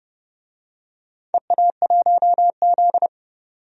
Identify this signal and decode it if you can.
Morse code, heard as a keyed tone, about 28 wpm, 700 Hz. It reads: EA1Z
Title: DXCC-100 Call Signs - 4 Characters - 30wpm